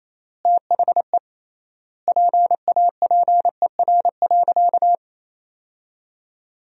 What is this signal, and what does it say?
Morse code, heard as a keyed tone, about 28 wpm, 700 Hz. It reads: THE PAPER.
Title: 3x Word-Spacing - Sentences from Top 300 Words - 28wpm